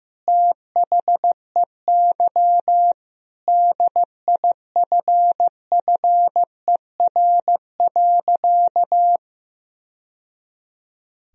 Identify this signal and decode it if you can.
Morse code, heard as a keyed tone, about 15 wpm, 700 Hz. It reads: THEY DIFFER.
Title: Sentences from Top 1000 Words - Review Part 1 - 15wpm